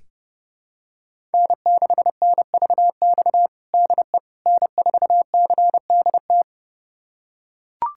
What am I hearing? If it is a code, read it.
Morse code, heard as a keyed tone, about 30 wpm, 700 Hz. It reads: N6NVX DE N4CDT
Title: USA Callsign DE Callsign - 5 Characters - 30wpm